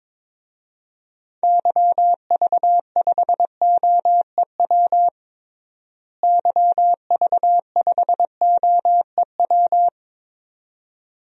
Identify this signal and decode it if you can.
Morse code, heard as a keyed tone, about 22 wpm, 700 Hz. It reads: YV5OEW YV5OEW
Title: DXCC-100 Call Signs - 6 Characters - Repeated 2x - 22wpm